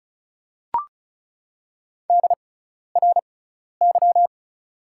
Morse code, heard as a keyed tone, about 34 wpm, spaced as for 12 wpm, 690 Hz. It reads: DRY
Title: 6x Character Spacing - Top 500 Words - 35wpm